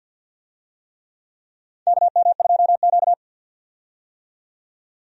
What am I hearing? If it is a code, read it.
Morse code, heard as a keyed tone, about 51 wpm, 700 Hz. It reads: KM2Q
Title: USA Call Signs - 4 Characters - 50wpm